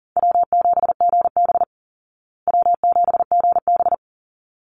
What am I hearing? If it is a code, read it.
Morse code, heard as a keyed tone, about 40 wpm, 700 Hz. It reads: W7GB W7GB
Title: Easy Call Signs Repeated 2x - 40wpm